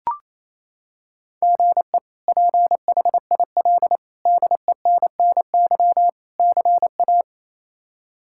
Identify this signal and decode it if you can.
Morse code, heard as a keyed tone, about 28 wpm, 700 Hz. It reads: GE PHIL DENNY CA
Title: K1USN SST QSO Practice - 28wpm